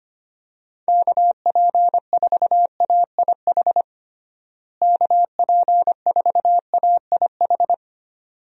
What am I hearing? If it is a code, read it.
Morse code, heard as a keyed tone, about 25 wpm, 700 Hz. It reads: KP4AIH KP4AIH
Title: DXCC-100 Call Signs - 6 Characters - Repeated 2x - 25wpm